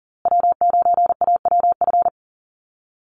Morse code, heard as a keyed tone, about 40 wpm, 700 Hz. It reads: W9AWF
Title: USA Call Signs Repeated 2x 40wpm